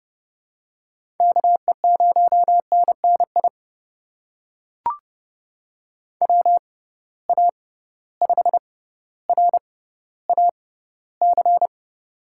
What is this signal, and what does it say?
Morse code, heard as a keyed tone, about 30 wpm, spaced as for 15 wpm, 700 Hz. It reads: KE0NNI WA5RAC
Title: USA Call Signs - 6 Characters - 6x Character Spacing - 30wpm